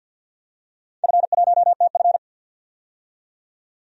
Morse code, heard as a keyed tone, about 52 wpm, 700 Hz. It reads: F1TF
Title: DXCC-100 Call Signs - 4 Characters - 50wpm